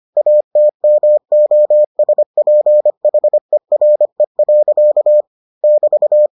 Morse code, heard as a keyed tone, about 25 wpm, 600 Hz. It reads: ATMOSPHERE. <BT>